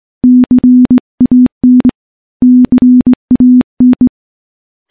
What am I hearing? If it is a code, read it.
Morse code, heard as a keyed tone, about 20 wpm, 250 Hz. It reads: CAN CAN
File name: morse / morse-training-20WPM